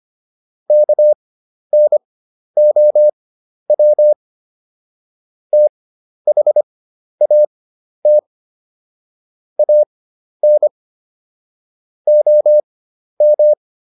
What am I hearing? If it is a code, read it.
Morse code, heard as a keyed tone, about 25 wpm, spaced as for 11 wpm, 600 Hz. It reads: KNOW THAT AN OM